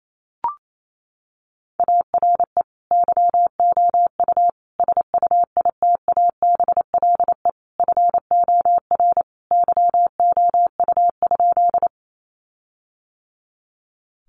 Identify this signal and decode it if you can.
Morse code, heard as a keyed tone, about 28 wpm, 700 Hz. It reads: ARE YOU SUITABLE FOR YOU?
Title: Sentences from Top 1400 Words 28wpm